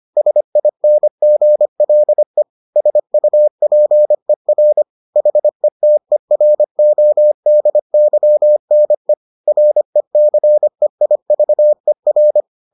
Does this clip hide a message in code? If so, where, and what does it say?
Yes: Morse code from 0.2 to 12.4 s, heard as a keyed tone, about 25 wpm, 600 Hz. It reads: SINGLE SUPER HETERODYNE RECEIVER